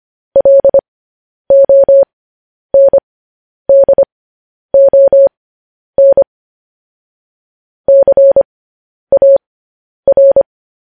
Morse code, heard as a keyed tone, about 25 wpm, spaced as for 10 wpm, 550 Hz. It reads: LONDON CAR
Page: news headlines in morse code at 10 wpm